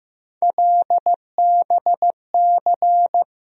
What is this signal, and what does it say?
Morse code, heard as a keyed tone, about 15 wpm, 700 Hz. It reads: LBC
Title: USA Call Sign Postfixes - 3 Letters 15wpm